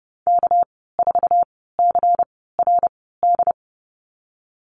Morse code, heard as a keyed tone, about 30 wpm, spaced as for 17 wpm, 700 Hz. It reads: K4CRD
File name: Lesson 37g - Letter x - Call Signs 30wpm